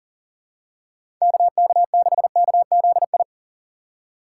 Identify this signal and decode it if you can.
Morse code, heard as a keyed tone, about 40 wpm, 700 Hz. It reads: KK6KZI
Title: USA Call Signs - 6 Characters - 4x Character Spacing - 40wpm